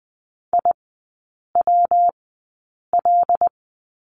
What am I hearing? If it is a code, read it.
Morse code, heard as a keyed tone, about 20 wpm, spaced as for 8 wpm, 700 Hz. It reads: IWL
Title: 114 Lesson 19e - Letter w - Three Characters 20wpm